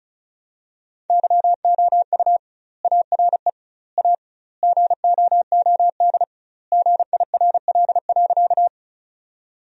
Morse code, heard as a keyed tone, about 36 wpm, 700 Hz. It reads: YOU ARE A GOOD GIRL.
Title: 2x Word-Spacing - Sentences from Top 400 Words - 35wpm